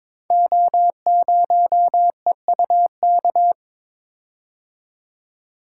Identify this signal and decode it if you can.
Morse code, heard as a keyed tone, about 22 wpm, 700 Hz. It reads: O0EUK
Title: Code Groups - 5 Letter:Numbers - 22wpm